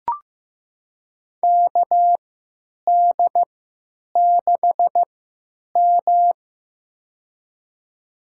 Morse code, heard as a keyed tone, about 15 wpm, spaced as for 8 wpm, 700 Hz. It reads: KD6M
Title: USA Call Signs - 4 Characters - 3x Character Spacing - 15wpm